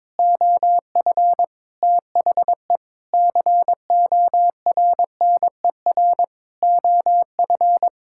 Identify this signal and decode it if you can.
Morse code, heard as a keyed tone, about 22 wpm, 700 Hz. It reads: OF THE CORNER OF
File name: Sentences with 7-Word Phrases - 22wpm